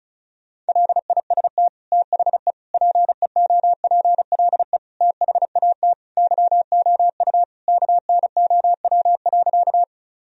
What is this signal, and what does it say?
Morse code, heard as a keyed tone, about 34 wpm, 700 Hz. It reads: LIST THE PEOPLE THAT YOU KNOW.
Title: Sentences from Top 1000 Words - Review Part 2 - 35wpm